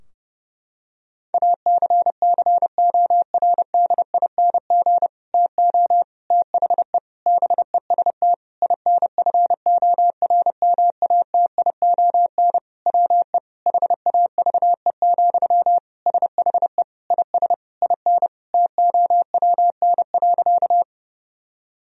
Morse code, heard as a keyed tone, about 30 wpm, 700 Hz. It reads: ACCORDING TO THE BEST INFORMATION WE HAVE, SHE IS IN TOWN.
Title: Sentences from Top 1000 Words - Review Part 4 - 30wpm